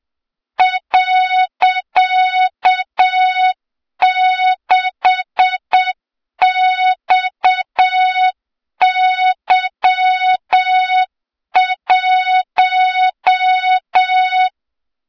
Morse code, heard as a keyed tone, about 7 wpm, 3700 Hz, with noise. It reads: .6XY1
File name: Morze-HA2MN / hang / 23 GYAKORLAT